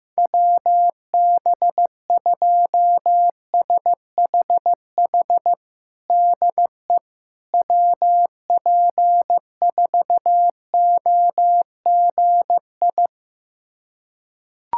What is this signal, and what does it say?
Morse code, heard as a keyed tone, about 15 wpm, 700 Hz. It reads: WB2SHH DE WP4OGI
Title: USA Callsign DE Callsign - 6 Characters - 15wpm